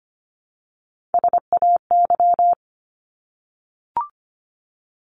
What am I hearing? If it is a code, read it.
Morse code, heard as a keyed tone, about 25 wpm, 700 Hz. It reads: SAY
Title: ES:S Suffix - Top 100 Words with Base Words 25wpm